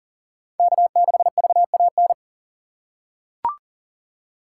Morse code, heard as a keyed tone, about 40 wpm, 700 Hz. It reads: K6VAN
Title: USA Call Signs - 5 Characters - 2x Character Spacing - 40wpm